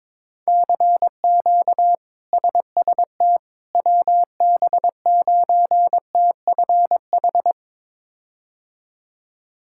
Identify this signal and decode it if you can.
Morse code, heard as a keyed tone, about 22 wpm, 700 Hz. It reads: CQ SST WB9TFH
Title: K1USN SST QSO Practice - 22wpm